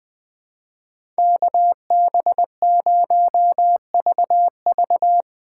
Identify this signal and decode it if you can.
Morse code, heard as a keyed tone, about 20 wpm, 700 Hz. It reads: KB0VV
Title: USA Call Signs - 5 Characters - 3x Character Spacing - 20wpm